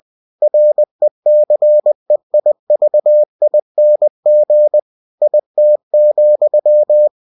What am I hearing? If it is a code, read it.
Morse code, heard as a keyed tone, about 20 wpm, 600 Hz. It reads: RECEIVING IT,
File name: cw-books / winnie-the-pooh / winnie-20wpm-page-0119